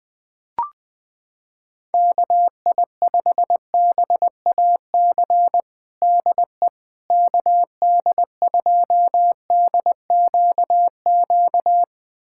Morse code, heard as a keyed tone, about 21 wpm, 700 Hz. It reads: KI5BAC DE KD2DQQ
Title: USA Callsign DE Callsign - 6 Characters - 20wpm